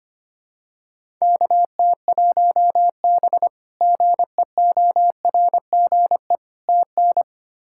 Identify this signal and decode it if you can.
Morse code, heard as a keyed tone, about 25 wpm, 700 Hz. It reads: KT1B GEORGE TN
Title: K1USN SST QSO Practice - 25wpm